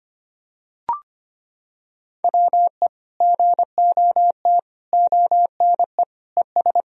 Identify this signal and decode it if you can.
Morse code, heard as a keyed tone, about 25 wpm, 700 Hz. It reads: WE GOT ONE ES
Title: Sentences from Top 300 Words and Abbreviations 25wpm